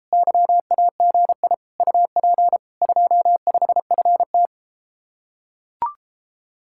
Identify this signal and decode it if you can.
Morse code, heard as a keyed tone, about 33 wpm, 700 Hz. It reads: YAGI UP 25FT